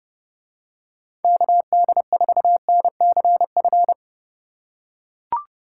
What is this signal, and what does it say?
Morse code, heard as a keyed tone, about 29 wpm, 700 Hz. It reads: KD4NCF